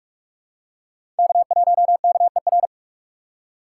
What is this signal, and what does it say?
Morse code, heard as a keyed tone, about 46 wpm, 700 Hz. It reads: K1KER
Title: USA Call Signs - 2x Character Spacing - 45wpm